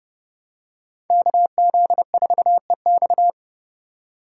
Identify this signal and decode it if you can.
Morse code, heard as a keyed tone, about 29 wpm, 700 Hz. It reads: KZ4EX